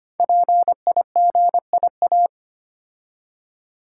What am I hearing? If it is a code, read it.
Morse code, heard as a keyed tone, about 25 wpm, 700 Hz. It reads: PIGIA
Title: Code Groups - 5 Letters - 25wpm